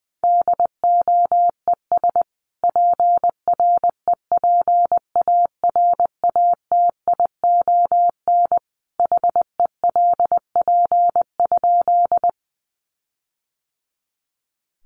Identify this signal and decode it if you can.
Morse code, heard as a keyed tone, about 19 wpm, 700 Hz. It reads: DOES PREPARATION HELP?
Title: Sentences from Top 1400 Words 20wpm